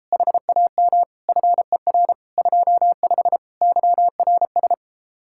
Morse code, heard as a keyed tone, about 33 wpm, 700 Hz. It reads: HAM FER 25 YRS